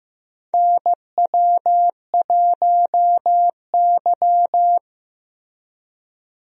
Morse code, heard as a keyed tone, about 16 wpm, 700 Hz. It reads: NW1Y
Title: USA Call Signs - 4 Characters - Repeated 2x - 15wpm